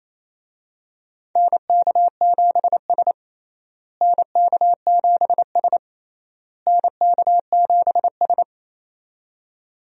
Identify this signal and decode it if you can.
Morse code, heard as a keyed tone, about 28 wpm, 700 Hz. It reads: NK7S NK7S NK7S
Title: USA Call Signs - 4 Characters - Repeated 3x - 28wpm